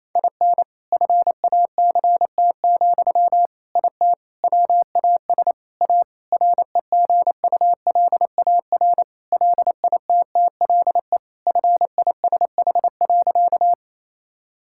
Morse code, heard as a keyed tone, about 28 wpm, 700 Hz. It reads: IN FACT, IT WAS A REGULAR LITTLE FISH.
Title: Sentences from Top 1000 Words - Review Part 6 - 28wpm